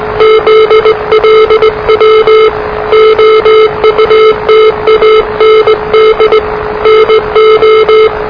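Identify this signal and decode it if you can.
Morse code, heard as a keyed tone, about 20 wpm, 440 Hz, with noise. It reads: ZLW OUT AND NO
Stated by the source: Morse challenge from NZ Net, the national amateur radio net of Aotearoa/New Zealand, which meets on 3535.0 kHz.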